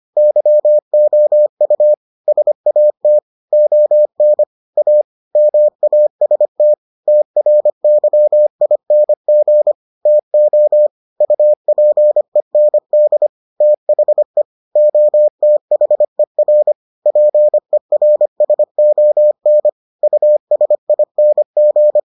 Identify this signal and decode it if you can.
Morse code, heard as a keyed tone, about 25 wpm, 600 Hz. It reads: YOU SAT ON A MAST TRYING TO UPEND THE OTHER PERSON USING